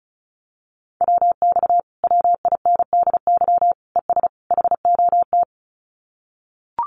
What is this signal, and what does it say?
Morse code, heard as a keyed tone, about 36 wpm, 700 Hz. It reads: WX WINDY ES HOT
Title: Common QSO Phrases Encore 35wpm